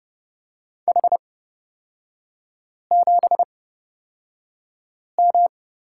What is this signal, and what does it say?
Morse code, heard as a keyed tone, about 31 wpm, spaced as for 5 wpm, 700 Hz. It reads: H7M
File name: Lesson 38e - Letter q - Three Characters 30wpm